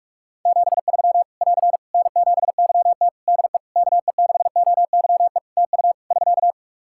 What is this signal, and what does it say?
Morse code, heard as a keyed tone, about 46 wpm, 700 Hz. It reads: 73 <AR> N7YT DE KE6OYE TU <SK>
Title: Practice QSO - Exchange 4 - Ending - 45wpm